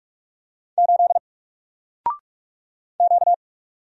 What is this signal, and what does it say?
Morse code, heard as a keyed tone, about 44 wpm, 700 Hz. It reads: 8 Q
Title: Single Letter-Number Alt 45wpm